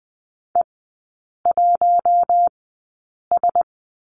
Morse code, heard as a keyed tone, about 20 wpm, spaced as for 8 wpm, 700 Hz. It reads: E1S
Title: 047 Lesson 9e - Number 4 - Three Characters 20wpm